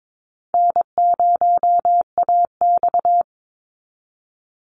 Morse code, heard as a keyed tone, about 22 wpm, 700 Hz. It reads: N0AX
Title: Easy Call Signs Repeated 3x - 22wpm